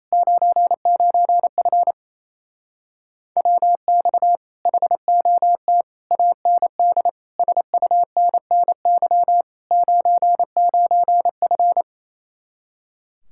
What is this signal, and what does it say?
Morse code, heard as a keyed tone, about 29 wpm, 700 Hz. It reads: 99F WX HOT AND SUNNY 99F
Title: Speed-Racing Common QSO Phrases Encore - 28wpm